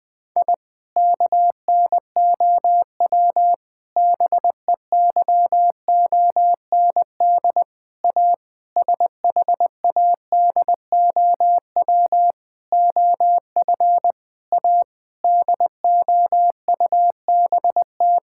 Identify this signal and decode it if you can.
Morse code, heard as a keyed tone, about 21 wpm, 700 Hz. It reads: I KNOW BEYOND A SHADOW OF A DOUBT
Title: Sentences with 7-Word Phrases - Spoken Beginning - 20wpm